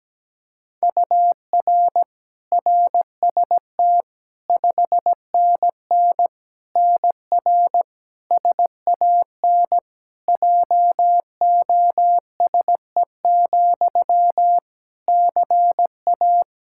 Morse code, heard as a keyed tone, about 17 wpm, 700 Hz. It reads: UR RST 5NN NR SAN JOSE, CA
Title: Practice QSO - Exchange 1 - Beginning - 17wpm